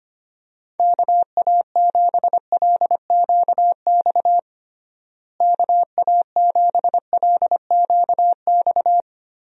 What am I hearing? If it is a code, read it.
Morse code, heard as a keyed tone, about 25 wpm, 700 Hz. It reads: KA7LQX KA7LQX
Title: USA Call Signs - 6 Characters - Repeated 2x - 25wpm